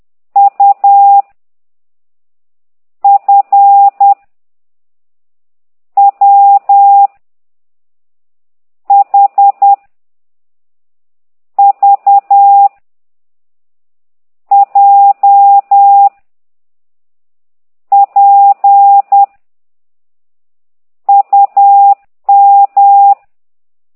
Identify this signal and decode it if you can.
Morse code, heard as a keyed tone, about 10 wpm, 800 Hz. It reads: U F W H V J P UM